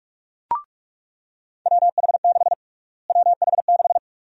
Morse code, heard as a keyed tone, about 46 wpm, 700 Hz. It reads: WH6 WH6
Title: USA Call Sign Prefixes - Repeated 2x - 45wpm